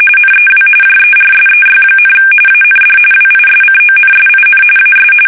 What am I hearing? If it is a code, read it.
RTTY, about 45.45 baud. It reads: NEW JERSEY, AND WISCONSIN AS
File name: fsk850 / USS-Massachusetts